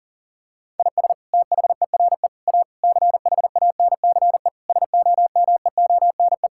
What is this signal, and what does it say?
Morse code, heard as a keyed tone, about 37 wpm, 700 Hz. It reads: IS THERE A CHANCE SOMEONE
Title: Sentences from Top 1000 Words - Review Part 6 - 40wpm